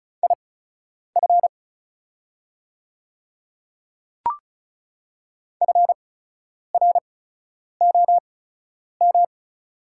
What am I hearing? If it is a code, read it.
Morse code, heard as a keyed tone, about 36 wpm, spaced as for 10 wpm, 700 Hz. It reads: IF FROM
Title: 8x Character Spacing - Top 100 Words - 35wpm